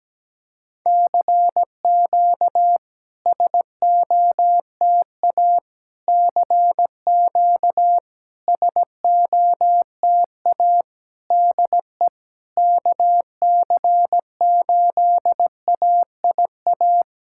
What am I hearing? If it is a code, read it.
Morse code, heard as a keyed tone, about 17 wpm, 700 Hz. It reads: CQ SOTA CQ SOTA DE KC8AIA
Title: SOTA QSO Practice 17wpm